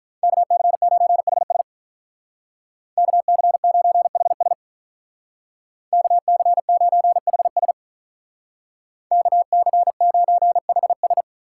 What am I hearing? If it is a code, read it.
Morse code, heard as a keyed tone, about 45 wpm, 700 Hz. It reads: KC9HS KC9HS KC9HS KC9HS